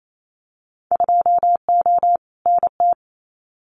Morse code, heard as a keyed tone, about 28 wpm, 700 Hz. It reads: 2O NT
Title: Winter Field Day Exchanges Encore 28wpm